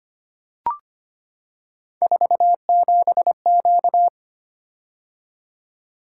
Morse code, heard as a keyed tone, about 25 wpm, 700 Hz. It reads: 47Q
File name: Three Letter-Number Combinations - 25wpm